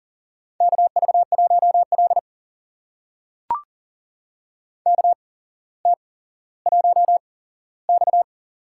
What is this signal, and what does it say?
Morse code, heard as a keyed tone, about 42 wpm, spaced as for 17 wpm, 700 Hz. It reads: KV1L KT1X
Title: USA Call Signs - 4 Characters - 8x Character Spacing - 40wpm